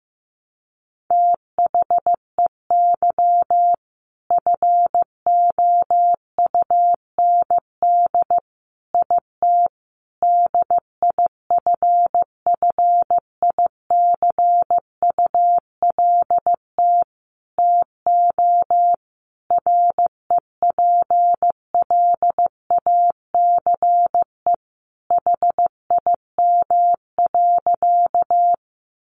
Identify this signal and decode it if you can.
Morse code, heard as a keyed tone, about 15 wpm, 700 Hz. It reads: THEY FOUND IT DIFFICULT TO REPLACE HIM.
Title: Sentences from Top 1400 Words 15wpm